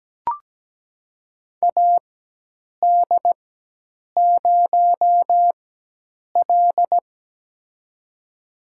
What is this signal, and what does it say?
Morse code, heard as a keyed tone, about 17 wpm, spaced as for 8 wpm, 700 Hz. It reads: AD0L
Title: USA Call Signs - 4 Characters - 4x Character Spacing - 17wpm